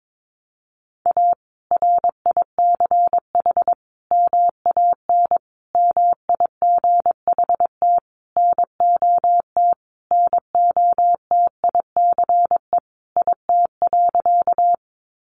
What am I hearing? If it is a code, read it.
Morse code, heard as a keyed tone, about 22 wpm, 700 Hz. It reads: A RICH MAN MIGHT NOT NOTICE IT.
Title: Sentences from Top 1100 Words 22wpm